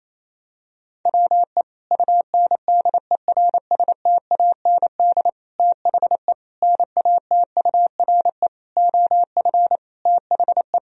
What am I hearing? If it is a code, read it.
Morse code, heard as a keyed tone, about 28 wpm, 700 Hz. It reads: WE UNDERSTAND THE NATURE OF THE
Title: Sentences from Top 1000 Words - Review Part 2 - 28wpm